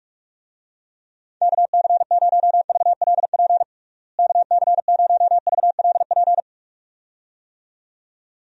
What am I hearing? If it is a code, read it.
Morse code, heard as a keyed tone, about 46 wpm, 700 Hz. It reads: KC0VLP KC0VLP
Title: USA Call Signs - 6 Characters - Repeated 2x - 45wpm